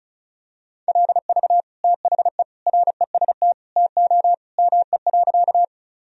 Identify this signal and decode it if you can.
Morse code, heard as a keyed tone, about 36 wpm, 700 Hz. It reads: LV THE REST TO ME.
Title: Sentences from Top 500 Words and Abbreviations 35wpm